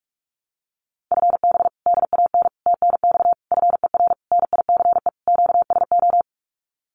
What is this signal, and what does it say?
Morse code, heard as a keyed tone, about 44 wpm, 700 Hz. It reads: FB DAN TNX FER NICE QSO
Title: Common QSO Phrases Encore 45wpm